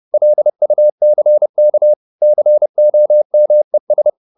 Morse code, heard as a keyed tone, about 30 wpm, 600 Hz. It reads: LUCK COMES
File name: the old man and the sea / the old man and the sea 1 30wpm